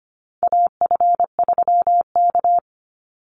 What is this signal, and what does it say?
Morse code, heard as a keyed tone, about 25 wpm, 700 Hz. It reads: AF3K
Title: Call Signs Repeated 3x - 25wpm